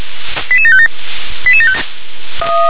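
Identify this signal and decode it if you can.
Five-tone selective call (ZVEI), heard as quick runs of five tones, tones of about 70 ms. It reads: 88757
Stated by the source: ZVEI five-tone selective call 88769 received on SDR.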